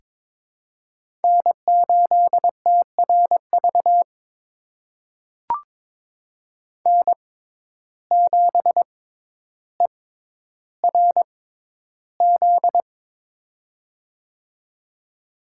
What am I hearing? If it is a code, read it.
Morse code, heard as a keyed tone, about 22 wpm, spaced as for 11 wpm, 700 Hz. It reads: N8TRV N7ERZ
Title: USA Call Signs - 5 Characters - 6x Character Spacing - 22wpm